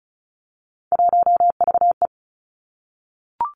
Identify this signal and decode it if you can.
Morse code, heard as a keyed tone, about 36 wpm, 700 Hz. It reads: 1VE
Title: Three Challenging Letter-Number Combinations 35wpm